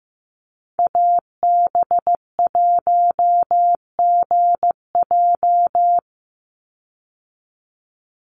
Morse code, heard as a keyed tone, about 15 wpm, 700 Hz. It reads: AB1GJ
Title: Call Signs Repeated 3x - 15wpm